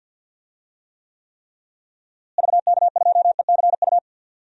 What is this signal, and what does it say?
Morse code, heard as a keyed tone, about 48 wpm, 700 Hz. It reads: VK2ECU
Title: DXCC-100 Call Signs - 6 Characters - Repeated 3x - 50wpm